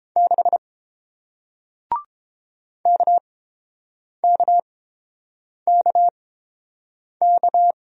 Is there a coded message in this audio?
Yes: Morse code from 0.2 to 7.7 s, heard as a keyed tone, about 30 wpm, spaced as for 7 wpm, 700 Hz. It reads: 6 KKKK